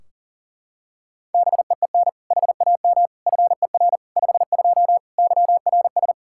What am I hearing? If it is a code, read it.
Morse code, heard as a keyed tone, about 40 wpm, 700 Hz. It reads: BEEN HAM FER 52 YRS
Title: Practice QSO - Exchange 3 - More Info - 40wpm